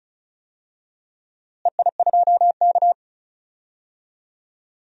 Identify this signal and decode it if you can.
Morse code, heard as a keyed tone, about 36 wpm, 700 Hz. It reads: EI2K
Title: DXCC-100 Call Signs - 4 Characters - Repeated 3x - 35wpm